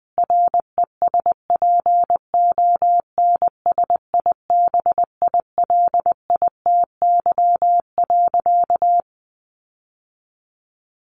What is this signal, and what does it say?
Morse code, heard as a keyed tone, about 19 wpm, 700 Hz. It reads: RESPONSIBILITY.
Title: Sentences from Top 1500 Words 20wpm